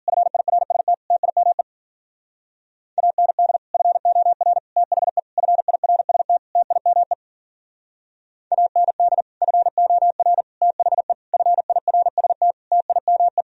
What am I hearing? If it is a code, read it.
Morse code, heard as a keyed tone, about 52 wpm, 700 Hz. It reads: FIRST TIME AND FOR THE FIRST TIME AND FOR THE FIRST TIME